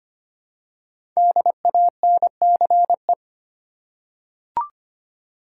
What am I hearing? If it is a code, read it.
Morse code, heard as a keyed tone, about 25 wpm, 700 Hz. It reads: DANCE
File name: 5 Letter Words - First 3 Letters Given - 25wpm